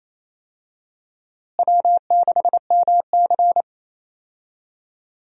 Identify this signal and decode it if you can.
Morse code, heard as a keyed tone, about 28 wpm, 700 Hz. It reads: W6MC
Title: USA Call Signs - 4 Characters - Repeated 2x - 28wpm